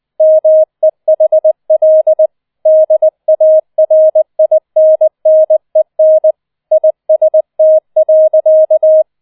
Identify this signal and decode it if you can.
Morse code, heard as a keyed tone, about 20 wpm, 630 Hz. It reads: MEHL DARINNEN IST.